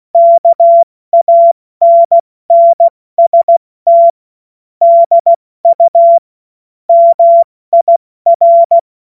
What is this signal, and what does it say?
Morse code, heard as a keyed tone, about 16 wpm, 680 Hz. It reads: KANNST DU MIR
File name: Morsekurs / Witze / Witz 03. August 2021, 70 F80